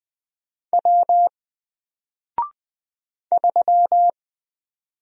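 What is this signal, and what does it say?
Morse code, heard as a keyed tone, about 20 wpm, 700 Hz. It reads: W 3
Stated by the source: Single letter and number Morse drill at 20 wpm, from N4LCD.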